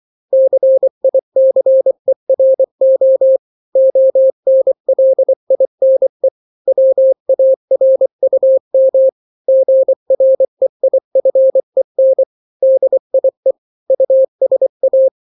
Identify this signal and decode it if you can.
Morse code, heard as a keyed tone, about 24 wpm, 530 Hz. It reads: CICERO ONLINE WARUM GREIFEN DIE USA
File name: daily / 23wpm